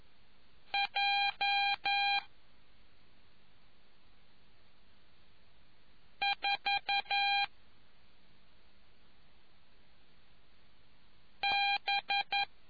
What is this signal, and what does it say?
Morse code, heard as a keyed tone, about 11 wpm, 800 Hz. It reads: J 4 B